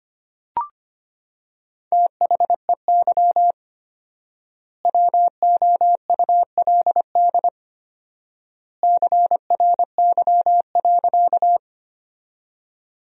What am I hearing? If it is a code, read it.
Morse code, heard as a keyed tone, about 25 wpm, 700 Hz. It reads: THEY WOULD CRY.
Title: 4x Word-Spacing - Sentences from Top 500 Words - 25wpm